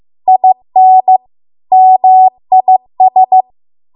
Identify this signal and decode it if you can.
Morse code, heard as a keyed tone, about 15 wpm, 750 Hz. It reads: IN MIS